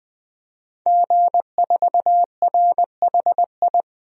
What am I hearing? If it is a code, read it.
Morse code, heard as a keyed tone, about 21 wpm, 700 Hz. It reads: G4RHI
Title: DXCC-100 Call Signs - 5 Characters - 20wpm